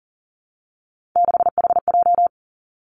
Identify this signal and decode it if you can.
Morse code, heard as a keyed tone, about 40 wpm, 700 Hz. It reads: 6HJ
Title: Three Challenging Letter-Number Combinations 40wpm